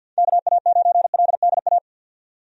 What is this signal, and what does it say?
Morse code, heard as a keyed tone, about 53 wpm, 700 Hz. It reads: KA9LDA
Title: USA Call Signs - 6 Characters - 50wpm